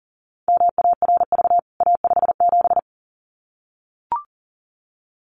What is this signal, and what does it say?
Morse code, heard as a keyed tone, about 40 wpm, 700 Hz. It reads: MARV A57
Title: CWT Exchange 40wpm